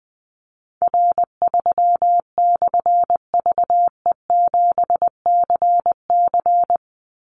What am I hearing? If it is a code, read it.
Morse code, heard as a keyed tone, about 20 wpm, 700 Hz. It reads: R3/VE7CC
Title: Call Signs - 20wpm